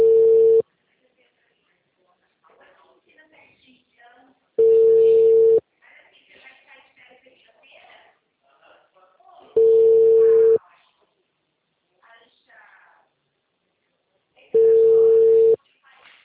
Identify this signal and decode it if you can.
Ringback tone (China, 450 Hz), with speech.